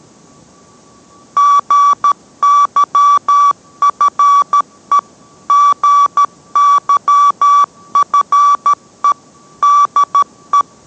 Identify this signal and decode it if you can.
Morse code, heard as a keyed tone, about 13 wpm, 1150 Hz. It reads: GYFE GYFE DE